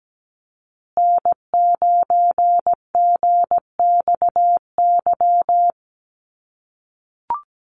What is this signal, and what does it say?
Morse code, heard as a keyed tone, about 17 wpm, 700 Hz. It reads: N9GXY
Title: USA Call Signs Repeated 3x 17wpm